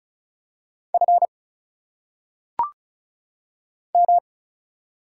Morse code, heard as a keyed tone, about 36 wpm, 700 Hz. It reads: F M
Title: Single Letters - Phonetic Pronunciation 35wpm